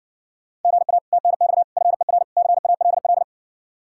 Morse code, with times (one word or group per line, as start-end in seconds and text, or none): 0.6-1.0 GA
1.1-1.6 TNX
1.8-2.2 FER
2.4-3.2 CALL